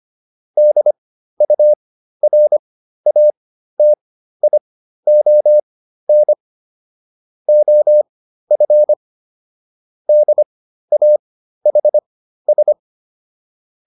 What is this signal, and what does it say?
Morse code, heard as a keyed tone, about 25 wpm, spaced as for 13 wpm, 600 Hz. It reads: DURATION OF DAHS